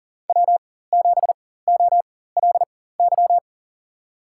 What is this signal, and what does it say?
Morse code, heard as a keyed tone, about 39 wpm, spaced as for 19 wpm, 700 Hz. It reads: W7OLY
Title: USA Call Signs - 5 Characters - 4x Character Spacing - 40wpm